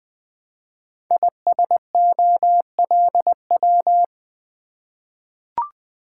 Morse code, heard as a keyed tone, about 20 wpm, 700 Hz. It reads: ISOLW
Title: Code Groups - 5 Letter:Numbers - 20wpm